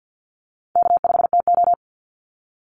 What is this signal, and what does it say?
Morse code, heard as a keyed tone, about 50 wpm, 700 Hz. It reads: K5TO
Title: Call Signs - 50wpm